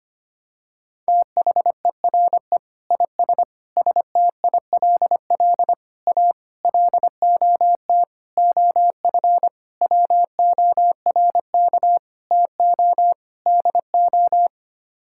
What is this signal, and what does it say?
Morse code, heard as a keyed tone, about 25 wpm, 700 Hz. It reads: THERE IS STILL A LOT OF WORK TO DO
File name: Sets of 9 Words - 25wpm